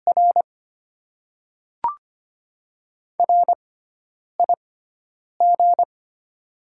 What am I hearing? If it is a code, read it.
Morse code, heard as a keyed tone, about 25 wpm, spaced as for 9 wpm, 700 Hz. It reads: R RIG
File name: Top 20 QSO Elements - 6x Character Spacing - 25wpm